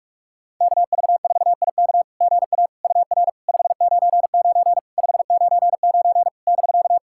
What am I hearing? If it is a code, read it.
Morse code, heard as a keyed tone, about 41 wpm, 690 Hz. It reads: KV4IK GA UR 599 599 <BK>
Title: SOTA QSO Practice 45wpm